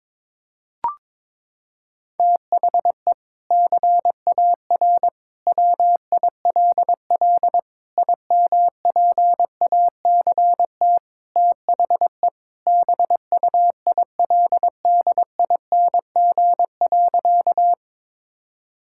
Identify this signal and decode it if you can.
Morse code, heard as a keyed tone, about 22 wpm, 700 Hz. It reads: THE CAR WILL IMPACT THE BUILDING.
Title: Sentences from Top 1000 Words - Review Part 6 - 22wpm